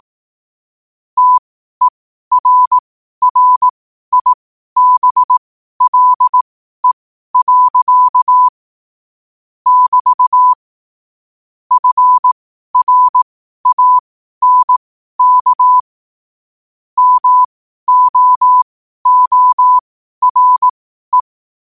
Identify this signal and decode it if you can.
Morse code, heard as a keyed tone, about 18 wpm, spaced as for 12 wpm, 1000 Hz. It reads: TERRIBLE. <BT> FRANK MOORE
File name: qotdfiles / 12wpm